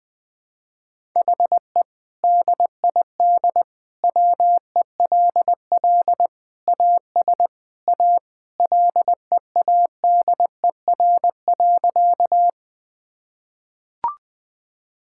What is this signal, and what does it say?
Morse code, heard as a keyed tone, about 21 wpm, 700 Hz. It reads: HE DID WELL AS A LEADER.
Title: Sentences with 2-Word Phrases - 20wpm